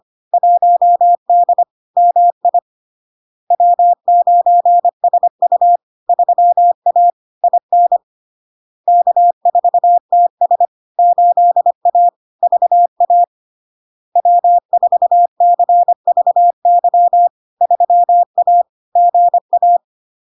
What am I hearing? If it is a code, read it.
Morse code, heard as a keyed tone, about 25 wpm, 700 Hz. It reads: 1D MI W9SU 3A IN K4TS 8A VA W4CVY 3A GA
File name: LARK-FD-2022-Log-in-CW-25wpm-700-Hz